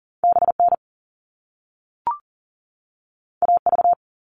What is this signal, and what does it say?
Morse code, heard as a keyed tone, about 40 wpm, 700 Hz. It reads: BN AV
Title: All Cut Numbers - 2-Digit 40wpm